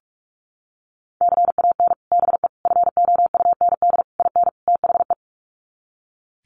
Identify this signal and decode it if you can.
Morse code, heard as a keyed tone, about 44 wpm, 700 Hz. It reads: CAN BE FOUND IN THE
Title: Sets of 5 Words Encore 45wpm